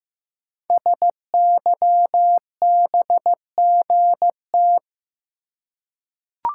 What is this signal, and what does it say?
Morse code, heard as a keyed tone, about 15 wpm, 700 Hz. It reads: SYBGT
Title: Code Groups - 5 Letters - 15wpm